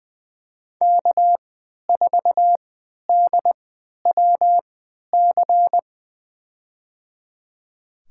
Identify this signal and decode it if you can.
Morse code, heard as a keyed tone, about 20 wpm, spaced as for 11 wpm, 700 Hz. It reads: K4DWC
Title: USA Call Signs - 3x Character Spacing - 20wpm